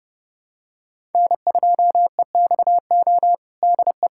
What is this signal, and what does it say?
Morse code, heard as a keyed tone, about 32 wpm, 700 Hz. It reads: N2EXO DE
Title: USA Callsign DE Callsign - 5 Characters - 30wpm